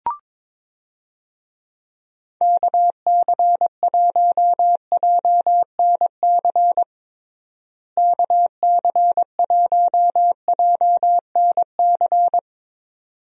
Morse code, heard as a keyed tone, about 22 wpm, 700 Hz. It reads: KC1JNC KC1JNC
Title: USA Call Signs - 6 Characters - Repeated 2x - 22wpm